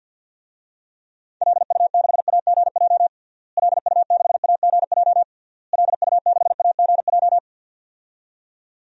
Morse code, heard as a keyed tone, about 51 wpm, 700 Hz. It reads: LU6AGJ LU6AGJ LU6AGJ
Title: DXCC-100 Call Signs - 6 Characters - Repeated 3x - 50wpm